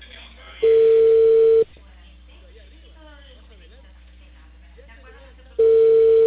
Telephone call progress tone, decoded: ringback tone

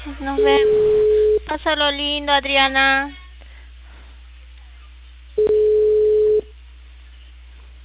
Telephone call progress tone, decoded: ringback tone

